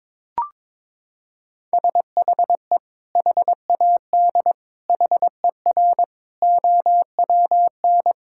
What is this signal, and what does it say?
Morse code, heard as a keyed tone, about 22 wpm, 700 Hz. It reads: SHE HAD HER OWN
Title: Sentences from Top 1000 Words - Review Part 5 - 22wpm